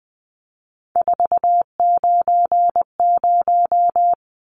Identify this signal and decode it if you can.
Morse code, heard as a keyed tone, about 20 wpm, 700 Hz. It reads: 490
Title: Numbers - 3-Digit 20wpm